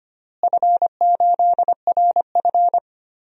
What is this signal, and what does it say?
Morse code, heard as a keyed tone, about 25 wpm, 700 Hz. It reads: F8RF
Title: DXCC-100 Call Signs - 4 Characters - Repeated 2x - 25wpm